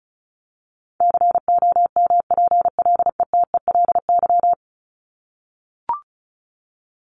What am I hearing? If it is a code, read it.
Morse code, heard as a keyed tone, about 34 wpm, 700 Hz. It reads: COMPLETELY
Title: Top 1001-1250 Words 35wpm